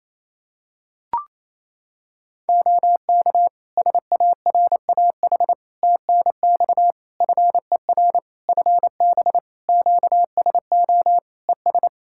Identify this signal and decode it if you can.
Morse code, heard as a keyed tone, about 28 wpm, 700 Hz. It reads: OK SARAH TNX FER FB QSO ES
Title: Practice QSO - Exchange 4 - Ending - 28wpm